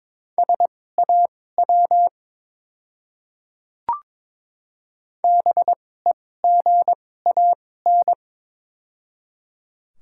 Morse code, heard as a keyed tone, about 22 wpm, spaced as for 16 wpm, 700 Hz. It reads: SAW BEGAN
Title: 2x Character Spacing - Top 300 Words - 22wpm